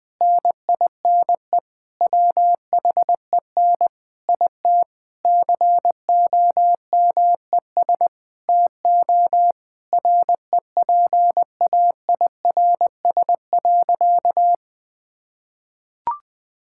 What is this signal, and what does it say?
Morse code, heard as a keyed tone, about 20 wpm, 700 Hz. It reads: NINE WHEN IT COMES TO REPAIRS.
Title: Difficult Sentences with Hints - 20wpm